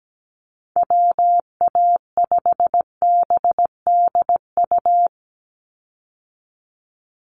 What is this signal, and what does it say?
Morse code, heard as a keyed tone, about 17 wpm, 700 Hz. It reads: WA5BDU
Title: Call Signs - 17wpm